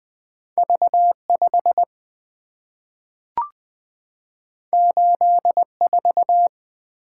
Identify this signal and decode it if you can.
Morse code, heard as a keyed tone, about 20 wpm, 700 Hz. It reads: V 5 8 4
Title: Two Letter-Number Combinations - 20wpm